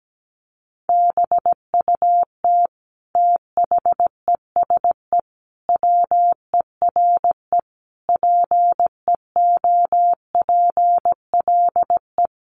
Morse code, heard as a keyed tone, about 17 wpm, 700 Hz. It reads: BUT THESE WERE PEOPLE
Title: Sentences from Top 1500 Words 17wpm